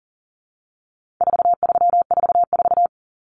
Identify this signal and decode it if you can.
Morse code, heard as a keyed tone, about 40 wpm, 700 Hz. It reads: 4344
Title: Common Cut Numbers - 4-Digit 40wpm